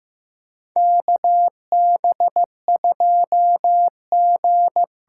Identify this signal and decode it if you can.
Morse code, heard as a keyed tone, about 15 wpm, 700 Hz. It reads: KB2G
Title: USA Call Signs - 6 Characters - 6x Character Spacing - 15wpm